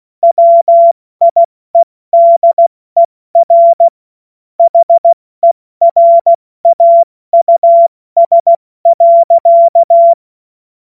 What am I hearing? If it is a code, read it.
Morse code, heard as a keyed tone, about 16 wpm, 680 Hz. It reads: WIEDER HERAUS.
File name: Witz-14-Oktober-2025-70-F80